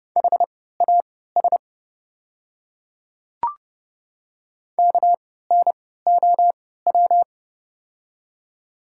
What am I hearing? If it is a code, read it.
Morse code, heard as a keyed tone, about 32 wpm, spaced as for 17 wpm, 700 Hz. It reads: HAS KNOW
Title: 3x Character Spacing - Top 100 Words - 30wpm